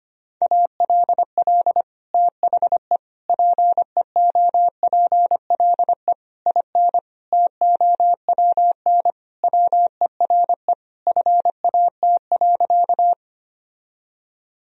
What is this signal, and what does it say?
Morse code, heard as a keyed tone, about 25 wpm, 700 Hz. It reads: ALL THE PEOPLE IN TOWN WERE FAT.
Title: Sentences from Top 1000 Words - Review Part 4 - 25wpm